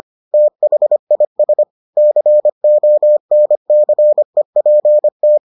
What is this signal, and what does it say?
Morse code, heard as a keyed tone, about 25 wpm, 600 Hz. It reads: THIS CONCEPT